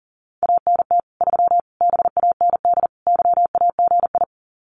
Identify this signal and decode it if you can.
Morse code, heard as a keyed tone, about 40 wpm, 700 Hz. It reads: ANT 3 BAND YAGI
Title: Common QSO Phrases 40wpm